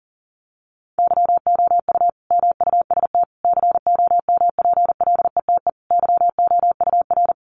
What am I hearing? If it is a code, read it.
Morse code, heard as a keyed tone, about 40 wpm, 700 Hz. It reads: YOU MUST COMPLETE YOUR
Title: Sentences from Top 1400 Words 40wpm